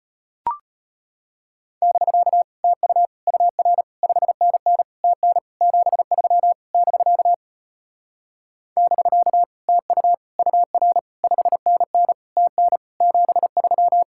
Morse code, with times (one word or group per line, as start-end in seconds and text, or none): 1.8-2.4 <BK>
2.6-3.1 TU
3.3-3.8 UR
4.0-4.8 5NN
5.0-5.4 TN
5.6-6.5 73
6.7-7.3 <BK>
8.8-9.4 <BK>
9.7-10.1 TU
10.4-11.0 UR
11.2-12.1 5NN
12.4-12.8 TN
13.0-14.0 73